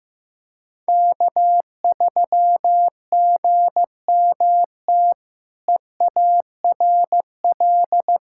Morse code, heard as a keyed tone, about 15 wpm, 700 Hz. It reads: K3GMT EARL
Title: K1USN SST QSO Practice - 15wpm